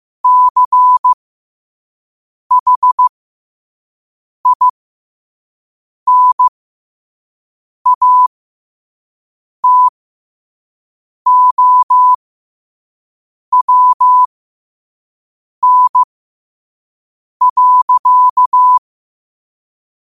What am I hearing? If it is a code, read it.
Morse code, heard as a keyed tone, about 15 wpm, spaced as for 5 wpm, 1000 Hz. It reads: CHINATOWN.
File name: qotdfiles / 5wpm